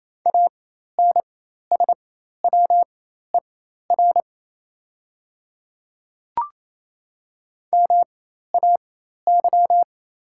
Morse code, heard as a keyed tone, about 28 wpm, spaced as for 13 wpm, 700 Hz. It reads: ANSWER MAY
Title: 4x Character Spacing - Top 300 Words - 28wpm